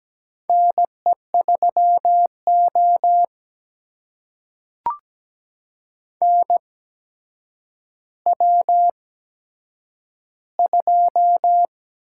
Morse code, heard as a keyed tone, about 17 wpm, spaced as for 8 wpm, 700 Hz. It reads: NE3O NW2
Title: USA Call Signs - 4 Characters - 8x Character Spacing - 17wpm